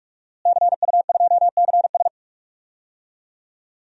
Morse code, heard as a keyed tone, about 46 wpm, 700 Hz. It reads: CU2CS
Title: DXCC-100 Call Signs - 5 Characters - Repeated 3x - 45wpm